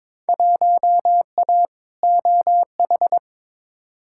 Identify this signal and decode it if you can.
Morse code, heard as a keyed tone, about 22 wpm, 700 Hz. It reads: 1A OH
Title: Field Day Exchanges - 22wpm